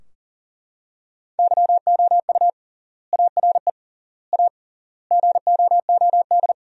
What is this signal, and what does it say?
Morse code, heard as a keyed tone, about 37 wpm, 700 Hz. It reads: YOU ARE A GOOD
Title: 3x Word-Spacing - Sentences from Top 400 Words - 40wpm